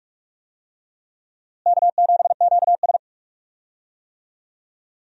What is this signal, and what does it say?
Morse code, heard as a keyed tone, about 47 wpm, 700 Hz. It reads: K7QS
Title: USA Call Signs - 4 Characters - Repeated 3x - 45wpm